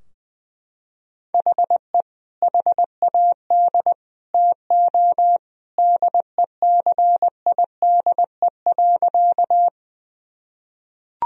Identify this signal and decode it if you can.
Morse code, heard as a keyed tone, about 20 wpm, 700 Hz. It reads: HE HAD TO DECIDE.
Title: Sentences from Top 1000 Words - Review Part 3 - 20wpm